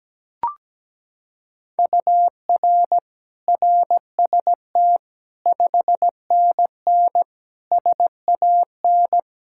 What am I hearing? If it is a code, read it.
Morse code, heard as a keyed tone, about 17 wpm, 700 Hz. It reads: UR RST 5NN SAN
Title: Practice QSO - Exchange 1 - Beginning - 17wpm